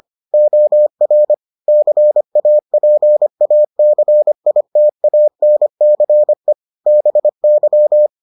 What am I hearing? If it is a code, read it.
Morse code, heard as a keyed tone, about 25 wpm, 600 Hz. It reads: OR CAPACITANCE BY